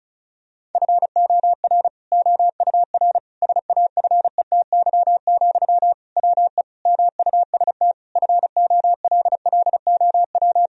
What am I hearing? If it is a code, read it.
Morse code, heard as a keyed tone, about 34 wpm, 700 Hz. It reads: FOR OUR SAFETY, WE MUST FOLLOW
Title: Sentences from Top 1000 Words - Review Part 4 - 35wpm